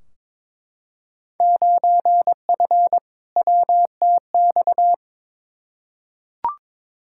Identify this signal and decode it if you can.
Morse code, heard as a keyed tone, about 22 wpm, 700 Hz. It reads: 9F WTX
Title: Field Day Exchanges Encore - 22wpm